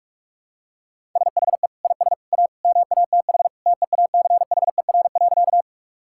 Morse code, heard as a keyed tone, about 41 wpm, 700 Hz. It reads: SHE IS A MATH TEACHER.
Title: Sentences from Top 1000 Words - Review Part 6 - 45wpm